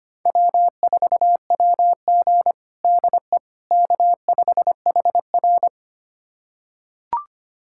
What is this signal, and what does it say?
Morse code, heard as a keyed tone, about 25 wpm, 700 Hz. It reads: W4WG DE K5HR
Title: USA Callsign DE Callsign - 4 Characters - 25wpm